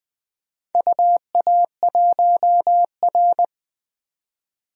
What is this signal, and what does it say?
Morse code, heard as a keyed tone, about 20 wpm, 700 Hz. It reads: UA1R
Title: DXCC-100 Call Signs - 4 Characters - 20wpm